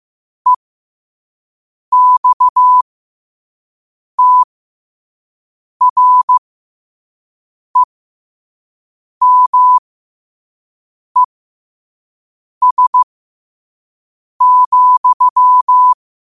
Morse code, heard as a keyed tone, about 15 wpm, spaced as for 5 wpm, 1000 Hz. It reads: EXTREMES,
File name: qotdfiles / 5wpm